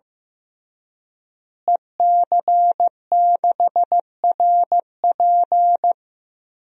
Morse code, heard as a keyed tone, about 15 wpm, 700 Hz. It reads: EC6RP
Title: DXCC-100 Call Signs - 5 Characters - Repeated 3x - 15wpm